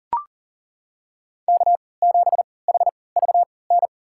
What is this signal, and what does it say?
Morse code, heard as a keyed tone, about 40 wpm, spaced as for 23 wpm, 700 Hz. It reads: K7HVN
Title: USA Call Signs - 3x Character Spacing - 40wpm